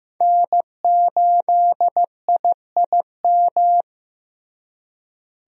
Morse code, heard as a keyed tone, about 16 wpm, 700 Hz. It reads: N8IIM
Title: USA Call Signs - 5 Characters - Repeated 2x - 15wpm